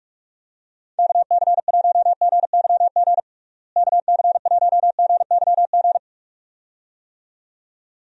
Morse code, heard as a keyed tone, about 44 wpm, 700 Hz. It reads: KC1GYG KC1GYG
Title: USA Call Signs - 6 Characters - Repeated 2x - 45wpm